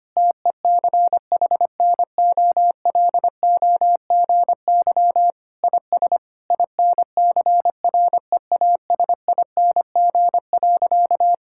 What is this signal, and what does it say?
Morse code, heard as a keyed tone, about 25 wpm, 700 Hz. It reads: TECHNOLOGY IS INCREASING.
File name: practice / 25wpm / Sentences with 2-Word Phrases - 25wpm